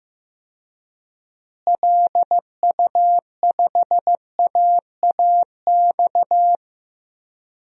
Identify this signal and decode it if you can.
Morse code, heard as a keyed tone, about 15 wpm, 700 Hz. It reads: LU5AAX
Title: DXCC-100 Call Signs - 6 Characters - Repeated 3x - 15wpm